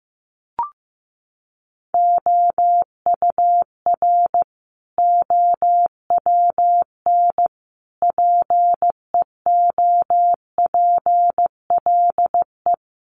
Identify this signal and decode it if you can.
Morse code, heard as a keyed tone, about 15 wpm, 700 Hz. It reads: OUR OWN PEOPLE
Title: Sentences from Top 300 Words 15wpm